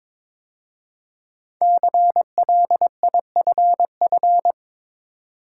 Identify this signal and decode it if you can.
Morse code, heard as a keyed tone, about 22 wpm, 700 Hz. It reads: CLIFF
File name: 5 Letter Words - First 3 Letters Given - 22wpm